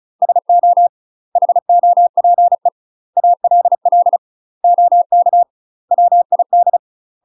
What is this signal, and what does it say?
Morse code, heard as a keyed tone, about 34 wpm, 700 Hz. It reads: SO HOPE ALL OK WID
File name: UK MASTER QSOs TRAINING / 35WPM / UK MASTER QSOs 0000